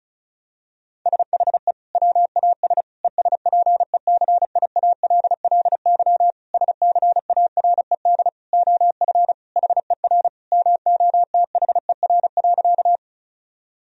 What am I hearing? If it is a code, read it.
Morse code, heard as a keyed tone, about 34 wpm, 700 Hz. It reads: SHE WAS ESPECIALLY SCARED OF HER MOTHER.
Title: Sentences from Top 1000 Words - Review Part 4 - 35wpm